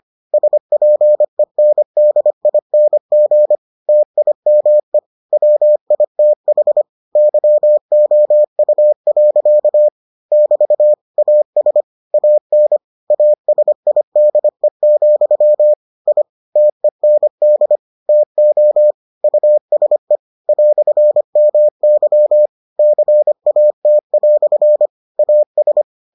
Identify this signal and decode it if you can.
Morse code, heard as a keyed tone, about 25 wpm, 600 Hz. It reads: SPENDING TIME WITH YOU. <BT> AS AN ASIDE, I TEND TO USE "MY CAT" AS